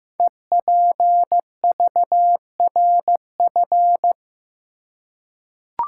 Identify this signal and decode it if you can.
Morse code, heard as a keyed tone, about 15 wpm, 700 Hz. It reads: EPVRF
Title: Code Groups - 5 Letters - 15wpm